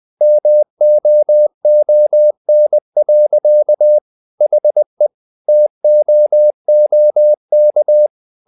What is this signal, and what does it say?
Morse code, heard as a keyed tone, about 20 wpm, 600 Hz. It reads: MOON. HE TOOK
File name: the old man and the sea / the old man and the sea part 1 20wpm